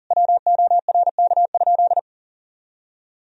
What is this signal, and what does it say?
Morse code, heard as a keyed tone, about 40 wpm, 700 Hz. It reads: WORK?
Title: Sentences from Top 500 Words and Abbreviations 40wpm